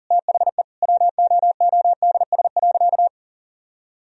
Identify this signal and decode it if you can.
Morse code, heard as a keyed tone, about 40 wpm, 700 Hz. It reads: THE WOODS.
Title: Sentences from Top 1000 Words - Review Part 2 - 40wpm